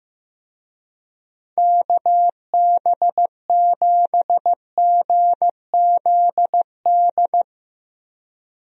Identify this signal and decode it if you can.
Morse code, heard as a keyed tone, about 15 wpm, 700 Hz. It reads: KB7GZD
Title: USA Call Signs - 6 Characters - Repeated 2x - 15wpm